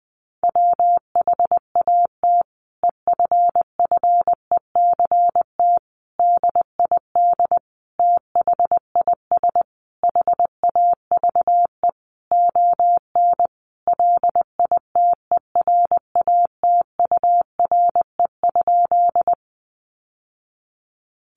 Morse code, heard as a keyed tone, about 19 wpm, 700 Hz. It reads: WHAT EFFECT DID THIS HAVE ON LITERATURE?
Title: Sentences from Top 1000 Words 20wpm